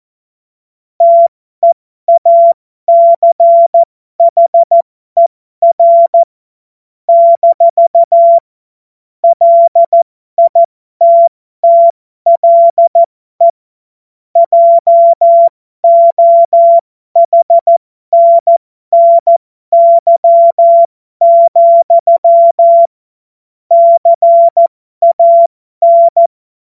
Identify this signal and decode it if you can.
Morse code, heard as a keyed tone, about 14 wpm, spaced as for 12 wpm, 680 Hz. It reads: TEACHER - LITTLE JOHNNY, CAN
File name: Morsekurs / Witze